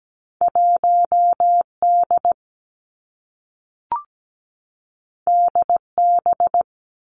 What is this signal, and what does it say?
Morse code, heard as a keyed tone, about 17 wpm, 700 Hz. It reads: 1D DB
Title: Two Challenging Letter-Number Combinations 17wpm